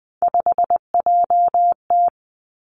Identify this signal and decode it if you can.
Morse code, heard as a keyed tone, about 20 wpm, 700 Hz. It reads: 5JT
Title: Three Challenging Letter-Number Combinations 20wpm